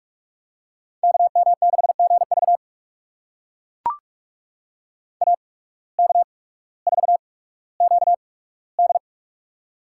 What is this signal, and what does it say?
Morse code, heard as a keyed tone, about 44 wpm, spaced as for 19 wpm, 690 Hz. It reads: KM6GV AK4QD
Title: USA Call Signs - 5 Characters - 8x Character Spacing - 45wpm